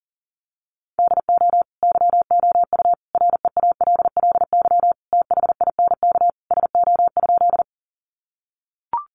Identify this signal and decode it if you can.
Morse code, heard as a keyed tone, about 40 wpm, 700 Hz. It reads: DO YOU REALLY THINK SO?
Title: Sentences from Top 600 Words 40wpm